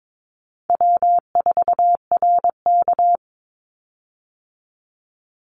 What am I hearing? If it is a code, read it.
Morse code, heard as a keyed tone, about 22 wpm, 700 Hz. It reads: W4RK
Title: Easy Call Signs - 22wpm